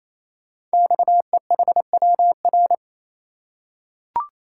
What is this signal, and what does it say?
Morse code, heard as a keyed tone, about 27 wpm, 700 Hz. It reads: XEHWR
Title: Code Groups - 5 Letters - 28wpm